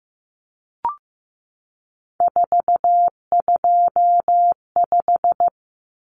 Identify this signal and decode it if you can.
Morse code, heard as a keyed tone, about 15 wpm, 700 Hz. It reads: 425
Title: Numbers - 3-Digit 15wpm